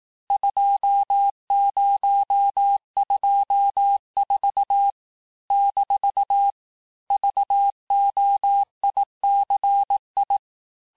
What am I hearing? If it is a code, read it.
Morse code, heard as a keyed tone, about 18 wpm, 800 Hz. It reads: 2024 - VOICI
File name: cours-du-lundi files / cours-1594--teste-chiffre-vitesse-17mpm